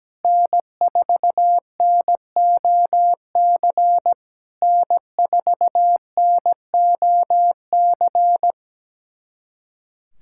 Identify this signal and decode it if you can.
Morse code, heard as a keyed tone, about 17 wpm, 700 Hz. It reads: N4NOC N4NOC
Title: SOTA QSO Practice 17wpm